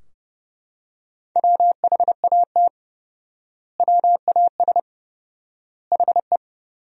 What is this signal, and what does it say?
Morse code, heard as a keyed tone, about 32 wpm, 700 Hz. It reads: WHAT WAS HE
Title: 4x Word-Spacing - Sentences from Top 100 Words - 30wpm